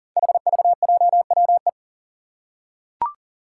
Morse code, heard as a keyed tone, about 39 wpm, 700 Hz. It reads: HVJWE
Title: Five Challenging Letter-Number Combinations 40wpm